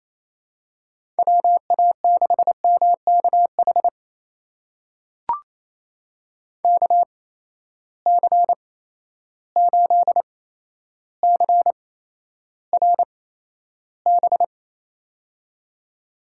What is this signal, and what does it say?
Morse code, heard as a keyed tone, about 28 wpm, spaced as for 12 wpm, 700 Hz. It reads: WA6MKH KC8CRB
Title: USA Call Signs - 6 Characters - 8x Character Spacing - 28wpm